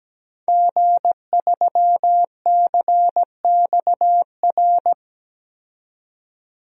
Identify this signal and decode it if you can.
Morse code, heard as a keyed tone, about 17 wpm, 700 Hz. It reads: G3CXR
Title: DXCC-100 Call Signs - 5 Characters - 17wpm